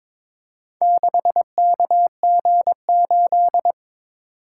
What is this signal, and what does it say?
Morse code, heard as a keyed tone, about 23 wpm, 700 Hz. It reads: 6KG8
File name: practice / 22wpm / Four Letter-Number Combinations - 22wpm